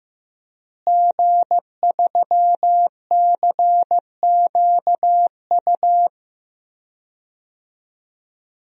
Morse code, heard as a keyed tone, about 15 wpm, 700 Hz. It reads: G3CQU
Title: DXCC-100 Call Signs - 5 Characters - Repeated 2x - 15wpm